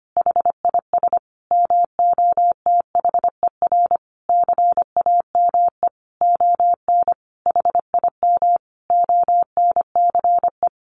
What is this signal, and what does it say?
Morse code, heard as a keyed tone, about 25 wpm, 700 Hz. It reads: HIS MOTHER CAME ON HIM ONCE
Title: Sentences from Top 1100 Words 25wpm